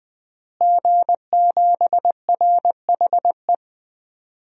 Morse code, heard as a keyed tone, about 20 wpm, 700 Hz. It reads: G7RHE
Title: DXCC-100 Call Signs - 5 Characters - Repeated 3x - 20wpm